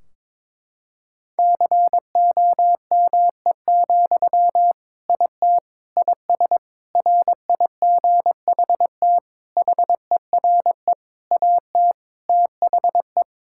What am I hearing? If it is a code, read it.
Morse code, heard as a keyed tone, about 22 wpm, 700 Hz. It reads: COME, IT IS RIGHT HERE AT THE
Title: Sentences from Top 1600 Words 22wpm